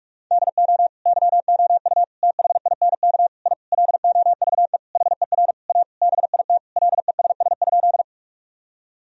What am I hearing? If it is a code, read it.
Morse code, heard as a keyed tone, about 46 wpm, 700 Hz. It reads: DO YOU THINK I LOVE HER A BIT LESS?
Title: Sentences from Top 1000 Words - Review Part 2 - 45wpm